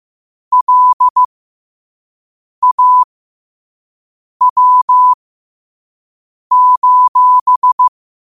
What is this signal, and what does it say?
Morse code, heard as a keyed tone, about 15 wpm, spaced as for 5 wpm, 1000 Hz. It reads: LAW:
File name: qotdfiles / 5wpm